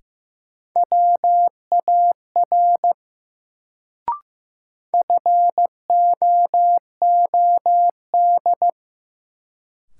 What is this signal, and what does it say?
Morse code, heard as a keyed tone, about 15 wpm, 700 Hz. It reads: WAR FOOD
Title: Sets of 5 Words - New Word Drill - 15wpm